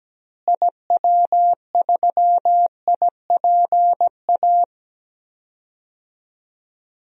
Morse code, heard as a keyed tone, about 17 wpm, 700 Hz. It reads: IW3IPA
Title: DXCC-100 Call Signs - 6 Characters - Repeated 3x - 17wpm